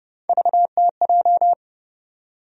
Morse code, heard as a keyed tone, about 31 wpm, 700 Hz. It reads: VTJ
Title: USA Call Sign Postfixes - 3 Letters 30wpm